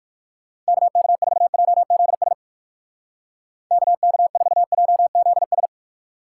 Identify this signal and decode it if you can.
Morse code, heard as a keyed tone, about 48 wpm, 700 Hz. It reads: KK4JZS KK4JZS
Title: Speed Racing - USA 2x3 Call Signs 45wpm